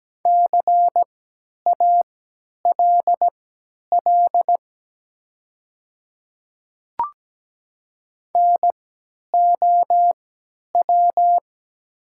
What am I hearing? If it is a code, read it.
Morse code, heard as a keyed tone, about 17 wpm, spaced as for 10 wpm, 700 Hz. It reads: CALL NOW
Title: 3x Character Spacing - Top 100 Words - 17wpm